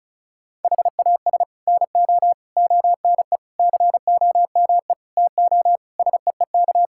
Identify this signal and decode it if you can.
Morse code, heard as a keyed tone, about 34 wpm, 700 Hz. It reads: HAS NO ONE COME TO SEEK
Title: Sentences from Top 1600 Words 35wpm